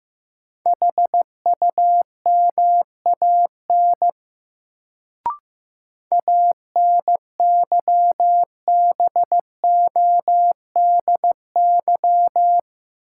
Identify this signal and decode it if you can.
Morse code, heard as a keyed tone, about 15 wpm, 700 Hz. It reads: HUMAN ANYBODY
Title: Sets of 5 Words - New Word Drill - 15wpm